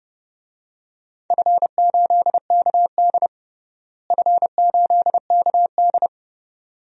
Morse code, heard as a keyed tone, about 30 wpm, 700 Hz. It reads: F8KD F8KD
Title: DXCC-100 Call Signs - 4 Characters - Repeated 2x - 30wpm